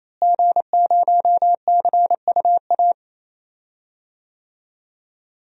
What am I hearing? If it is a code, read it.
Morse code, heard as a keyed tone, about 28 wpm, 700 Hz. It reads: G0CUA
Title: DXCC-100 Call Signs - 5 Characters - Repeated 3x - 28wpm